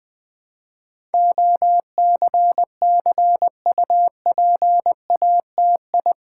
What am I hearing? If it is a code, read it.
Morse code, heard as a keyed tone, about 20 wpm, 700 Hz. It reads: OCCUPATI
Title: Build A Word - 4 Letters At A Time - 10-Letter Words - 20wpm